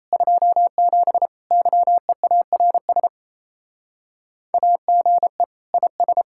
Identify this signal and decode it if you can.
Morse code, heard as a keyed tone, about 32 wpm, 700 Hz. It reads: 27 YEARS AGE IS
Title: Speed-Racing Common QSO Phrases Encore - 28wpm